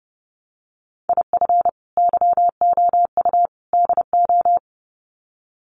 Morse code, heard as a keyed tone, about 30 wpm, 700 Hz. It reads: IF YOU DO
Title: Sets of 3 Words 30wpm